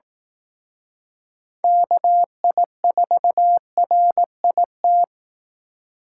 Morse code, heard as a keyed tone, about 19 wpm, 700 Hz. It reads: KI4RIT